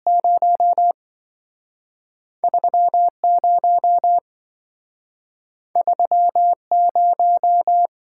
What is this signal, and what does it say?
Morse code, heard as a keyed tone, about 27 wpm, spaced as for 5 wpm, 700 Hz. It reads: 03 03 0